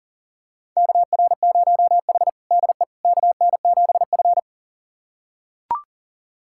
Morse code, heard as a keyed tone, about 37 wpm, 700 Hz. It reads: KR0H DE KN7F
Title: USA Callsign DE Callsign - 4 Characters - 40wpm